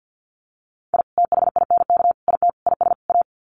Morse code, heard as a keyed tone, about 50 wpm, 700 Hz. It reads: I THINK IT IS A
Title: Sets of 5 Words Encore 50wpm